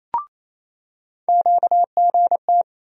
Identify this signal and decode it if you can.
Morse code, heard as a keyed tone, about 28 wpm, 700 Hz. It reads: QGT
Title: USA Call Sign Postfixes - 3 Letters 28wpm